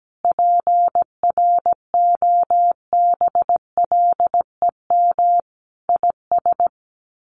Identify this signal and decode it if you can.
Morse code, heard as a keyed tone, about 17 wpm, 700 Hz. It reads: PROBLEM IS
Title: Sets of 5 Words Encore2 17wpm